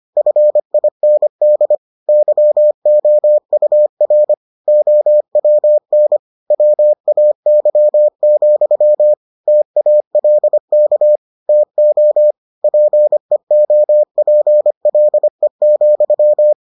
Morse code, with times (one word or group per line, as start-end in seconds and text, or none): 0.2-1.8 FIND
2.1-4.3 YOUR
4.7-6.2 OWN
6.5-9.1 WAY,
9.5-11.2 TALK
11.5-12.3 TO
12.6-16.5 PEOPLE,